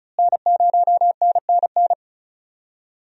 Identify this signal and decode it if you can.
Morse code, heard as a keyed tone, about 36 wpm, 700 Hz. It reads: N0NNN
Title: USA Call Signs - 5 Characters - Repeated 3x - 35wpm